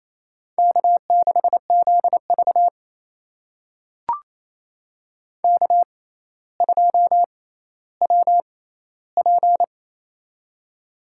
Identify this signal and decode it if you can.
Morse code, heard as a keyed tone, about 28 wpm, spaced as for 14 wpm, 700 Hz. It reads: K6ZV K2WP
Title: USA Call Signs - 4 Characters - 6x Character Spacing - 28wpm